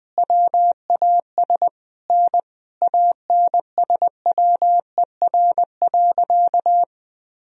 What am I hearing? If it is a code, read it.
Morse code, heard as a keyed tone, about 21 wpm, 700 Hz. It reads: WAS N ANSWER.
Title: Sentences from Top 300 Words and Abbreviations 20wpm